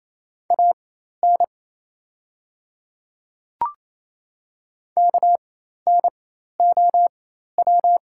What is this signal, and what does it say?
Morse code, heard as a keyed tone, about 28 wpm, spaced as for 13 wpm, 700 Hz. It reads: AN KNOW
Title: 4x Character Spacing - Top 100 Words - 28wpm